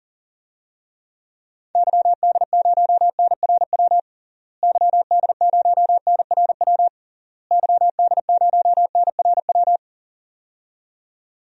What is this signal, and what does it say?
Morse code, heard as a keyed tone, about 40 wpm, 700 Hz. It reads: YD0NRW YD0NRW YD0NRW
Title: DXCC-100 Call Signs - 6 Characters - Repeated 3x - 40wpm